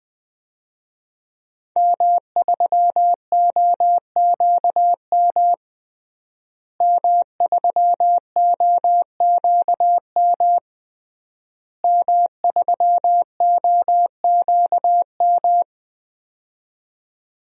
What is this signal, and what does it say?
Morse code, heard as a keyed tone, about 20 wpm, 700 Hz. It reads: M3OQM M3OQM M3OQM
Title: DXCC-100 Call Signs - 5 Characters - Repeated 3x - 20wpm